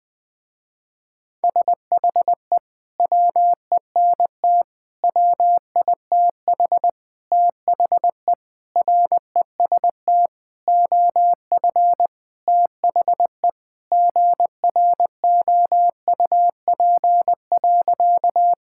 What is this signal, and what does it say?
Morse code, heard as a keyed tone, about 20 wpm, 700 Hz. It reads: SHE WENT WITH THE REST OF THE GROUP.
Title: Sentences with 5-Word Phrases - Spoken Beginning - 20wpm